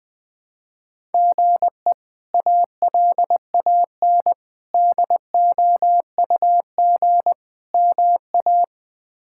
Morse code, heard as a keyed tone, about 20 wpm, 700 Hz. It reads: GE ALAN DOUG MA
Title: K1USN SST QSO Practice - 20wpm